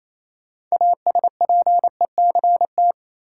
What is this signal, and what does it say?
Morse code, heard as a keyed tone, about 27 wpm, 700 Hz. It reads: ASPECT
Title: Sentences from Top 1300 Words - New Word Drill - 28wpm